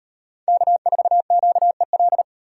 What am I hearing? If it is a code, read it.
Morse code, heard as a keyed tone, about 38 wpm, 700 Hz. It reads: K4QEL